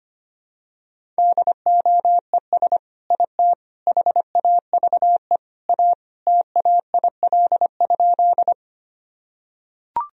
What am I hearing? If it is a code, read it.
Morse code, heard as a keyed tone, about 25 wpm, 700 Hz. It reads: DOES IT HAVE A TAIL?
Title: Sentences from Top 1000 Words - Review Part 3 - 25wpm